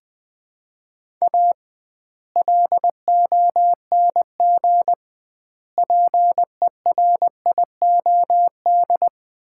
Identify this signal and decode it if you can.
Morse code, heard as a keyed tone, about 20 wpm, 700 Hz. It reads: A LONG PERIOD
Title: 2x Word-Spacing - Sets of 5 Words - 20wpm